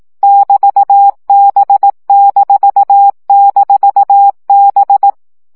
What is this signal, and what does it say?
Morse code, heard as a keyed tone, about 18 wpm, 800 Hz. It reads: <BT>B--B
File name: CW gyakorlas b 6 egyenlo minusz 15 baud